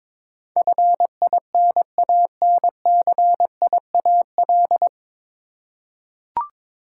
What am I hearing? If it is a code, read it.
Morse code, heard as a keyed tone, about 22 wpm, 700 Hz. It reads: FINANCIAL
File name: practice / 22wpm / Nine Letter Words - 22wpm